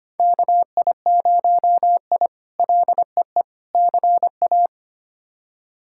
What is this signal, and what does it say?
Morse code, heard as a keyed tone, about 25 wpm, 700 Hz. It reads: KI0I LEE CA
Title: K1USN SST QSO Practice - 25wpm